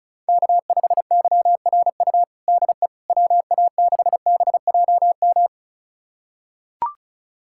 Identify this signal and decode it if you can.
Morse code, heard as a keyed tone, about 34 wpm, 700 Hz. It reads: K5YRU DE WA6BJM
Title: USA Callsign DE Callsign 35wpm